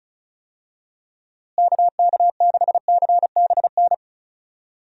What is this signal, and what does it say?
Morse code, heard as a keyed tone, about 36 wpm, 700 Hz. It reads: KK6CBN
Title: USA Call Signs - 6 Characters - Repeated 2x - 35wpm